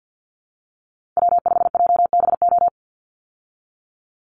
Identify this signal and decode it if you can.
Morse code, heard as a keyed tone, about 50 wpm, 700 Hz. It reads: W5JBO
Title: Call Signs Repeated 2x - 50wpm